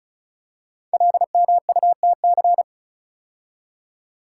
Morse code, heard as a keyed tone, about 36 wpm, 690 Hz. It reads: LMUTC
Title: Code Groups - 5 Letters - 35wpm